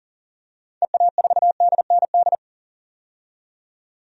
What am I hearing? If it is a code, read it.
Morse code, heard as a keyed tone, about 39 wpm, 700 Hz. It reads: EA4DND
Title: DXCC-100 Call Signs - 6 Characters - 40wpm